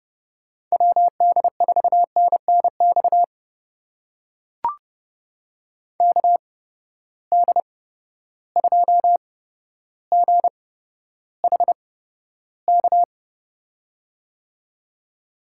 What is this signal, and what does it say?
Morse code, heard as a keyed tone, about 30 wpm, spaced as for 13 wpm, 700 Hz. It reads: WD4NNX KD2GHK
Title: USA Call Signs - 6 Characters - 8x Character Spacing - 30wpm